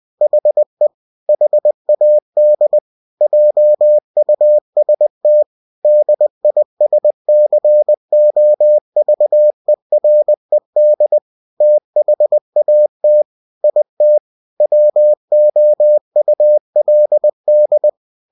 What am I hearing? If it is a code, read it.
Morse code, heard as a keyed tone, about 20 wpm, 600 Hz. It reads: HE HAD JUST DISCOVERED THAT IT WOULD
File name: cw-books / winnie-the-pooh / winnie-20wpm-page-0078